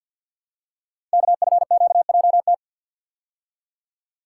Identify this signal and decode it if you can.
Morse code, heard as a keyed tone, about 47 wpm, 700 Hz. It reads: KFQJT
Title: Code Groups - 5 Letters - 50wpm